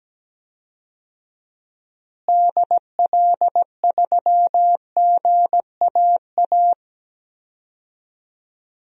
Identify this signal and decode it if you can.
Morse code, heard as a keyed tone, about 17 wpm, 700 Hz. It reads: DL3GAA
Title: DXCC-100 Call Signs - 6 Characters - Repeated 3x - 17wpm